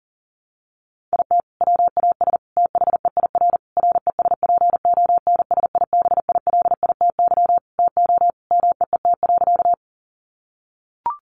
Morse code, heard as a keyed tone, about 40 wpm, 700 Hz. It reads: IT WAS THEIR RESPONSIBILITY TO MEET.
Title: Sentences from Top 1200 Words 40wpm